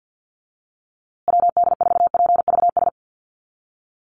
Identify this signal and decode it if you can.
Morse code, heard as a keyed tone, about 50 wpm, 700 Hz. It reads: WD4PVS
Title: USA Call Signs Repeated 2x 50wpm